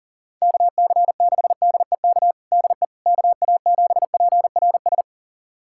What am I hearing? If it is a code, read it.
Morse code, heard as a keyed tone, about 40 wpm, 700 Hz. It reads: KC6DEK DE KA7PRS
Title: USA Callsign DE Callsign - 6 Characters - 40wpm